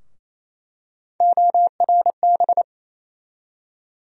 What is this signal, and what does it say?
Morse code, heard as a keyed tone, about 28 wpm, 700 Hz. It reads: ORB
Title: USA Call Sign Postfixes - 3 Letters 28wpm